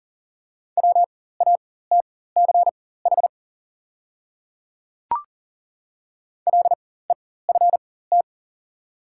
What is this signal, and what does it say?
Morse code, heard as a keyed tone, about 39 wpm, spaced as for 18 wpm, 700 Hz. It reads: WATCH LEFT
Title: 4x Character Spacing - Top 500 Words - 40wpm